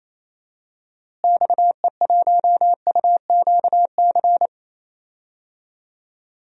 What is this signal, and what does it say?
Morse code, heard as a keyed tone, about 28 wpm, 700 Hz. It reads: XE1UQC
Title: DXCC-100 Call Signs - 6 Characters - Repeated 3x - 28wpm